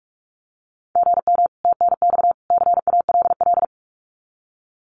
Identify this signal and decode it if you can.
Morse code, heard as a keyed tone, about 44 wpm, 700 Hz. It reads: GM TNX CALL
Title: Common QSO Phrases 45wpm